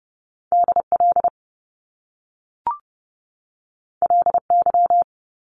Morse code, heard as a keyed tone, about 30 wpm, 700 Hz. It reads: DL LY
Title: Top Two Letter Combinations in Words 30wpm